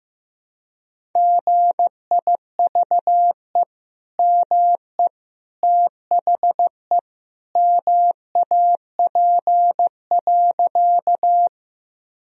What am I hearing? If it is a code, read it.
Morse code, heard as a keyed tone, about 15 wpm, 700 Hz. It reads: GIVE ME THE MAP.
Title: Sentences from Top 1000 Words - Review Part 2 - 15wpm